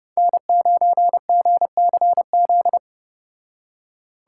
Morse code, heard as a keyed tone, about 29 wpm, 700 Hz. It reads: N9GCZ